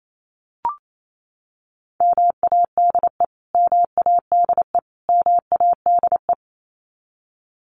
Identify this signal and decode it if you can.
Morse code, heard as a keyed tone, about 28 wpm, 700 Hz. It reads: MADE MADE MADE
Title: Top 300 Words Repeated 28wpm